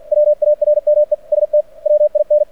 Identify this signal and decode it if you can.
Morse code, heard as a keyed tone, about 49 wpm, 600 Hz. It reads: WARMEST WIN
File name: archived / CRSnet